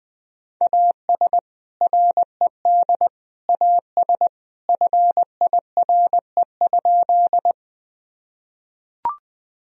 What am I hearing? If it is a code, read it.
Morse code, heard as a keyed tone, about 20 wpm, 700 Hz. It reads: AS RED AS FIRE?
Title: Sentences from Top 1000 Words - Review Part 2 - 20wpm